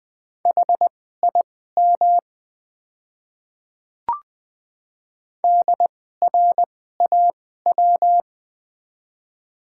Morse code, heard as a keyed tone, about 20 wpm, spaced as for 14 wpm, 700 Hz. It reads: HIM DRAW
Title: 2x Character Spacing - Top 500 Words - 20wpm